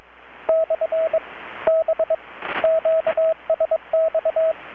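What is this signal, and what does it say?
Morse code, heard as a keyed tone, about 21 wpm, 640 Hz, with radio noise. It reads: / B QSX
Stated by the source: Morse from KPH's Night of Nights XX maritime radiotelegraphy event, received in Chiapas, Mexico about 3,800 km away on a Kenwood R-600 with a grounded 10 m long-wire antenna.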